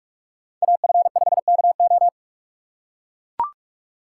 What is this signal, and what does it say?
Morse code, heard as a keyed tone, about 42 wpm, 700 Hz. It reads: AF5KO